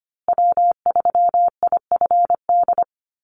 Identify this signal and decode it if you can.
Morse code, heard as a keyed tone, about 25 wpm, 700 Hz. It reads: W3IFD
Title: USA Call Signs Repeated 2x 25wpm